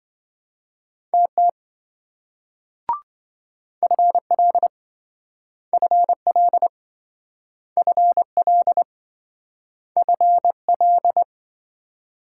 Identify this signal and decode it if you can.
Morse code, heard as a keyed tone, about 26 wpm, 700 Hz. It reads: TT FL FL FL FL